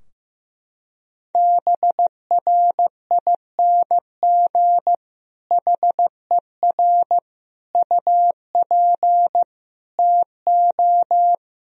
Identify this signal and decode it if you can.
Morse code, heard as a keyed tone, about 14 wpm, 700 Hz. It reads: BRING HER UP TO
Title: Sentences from Top 1000 Words - Review Part 3 - 15wpm